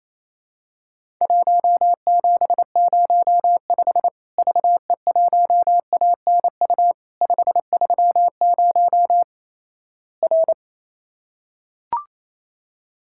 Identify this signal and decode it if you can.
Morse code, heard as a keyed tone, about 28 wpm, 700 Hz. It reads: 1705 VE1ANU 530 R
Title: Worked All Europe DX Contest - QTC Exchange - 28wpm